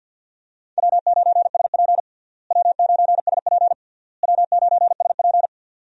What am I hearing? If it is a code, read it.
Morse code, heard as a keyed tone, about 51 wpm, 700 Hz. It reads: W9SP W9SP W9SP
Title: USA Call Signs - 4 Characters - Repeated 3x - 50wpm